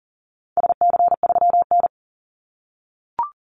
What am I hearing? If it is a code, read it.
Morse code, heard as a keyed tone, about 40 wpm, 700 Hz. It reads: SC3N
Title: Call Signs - 40wpm